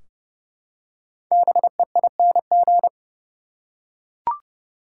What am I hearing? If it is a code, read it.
Morse code, heard as a keyed tone, about 30 wpm, 700 Hz. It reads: BEING
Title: Five Letter Words - 30wpm